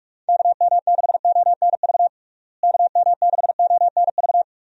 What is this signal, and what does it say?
Morse code, heard as a keyed tone, about 46 wpm, 700 Hz. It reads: KM6ONV KM6ONV
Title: USA Call Signs - 6 Characters - Repeated 3x - 45wpm